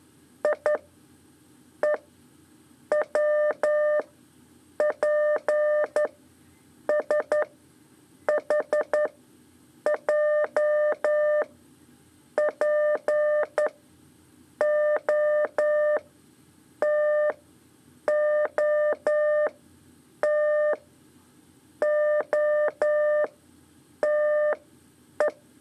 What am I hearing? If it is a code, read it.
Morse code, heard as a keyed tone, about 11 wpm, 600 Hz. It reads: I E W P S H J P O T O T O T E
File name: 3 lezione CW 2 AIR